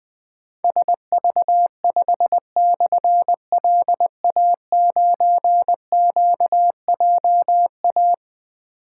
Morse code, heard as a keyed tone, about 20 wpm, 700 Hz. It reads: SV5/LA9QJA
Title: Difficult Call Signs 20wpm